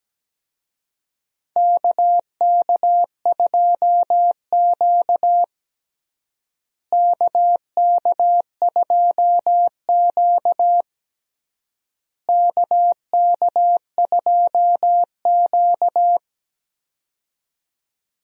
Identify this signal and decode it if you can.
Morse code, heard as a keyed tone, about 17 wpm, 700 Hz. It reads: KK2Q KK2Q KK2Q
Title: USA Call Signs - 4 Characters - Repeated 3x - 17wpm